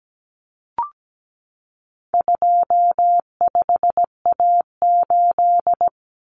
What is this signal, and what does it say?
Morse code, heard as a keyed tone, about 17 wpm, 700 Hz. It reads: 25A8
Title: Common Cut Numbers - 4-Digit 17wpm